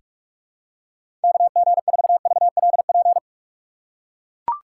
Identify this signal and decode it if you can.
Morse code, heard as a keyed tone, about 43 wpm, 700 Hz. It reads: KG4VLP
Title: USA Call Signs - 6 Characters - 3x Character Spacing - 45wpm